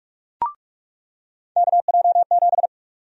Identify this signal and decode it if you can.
Morse code, heard as a keyed tone, about 46 wpm, 700 Hz. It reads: KJ7
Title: USA Call Sign Prefixes - 45wpm